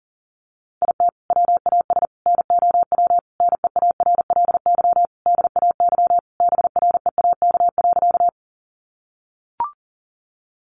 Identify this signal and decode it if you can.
Morse code, heard as a keyed tone, about 40 wpm, 700 Hz. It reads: IT WAS NOW NEARLY DAY BREAK.
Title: Sentences from Top 1300 Words 40wpm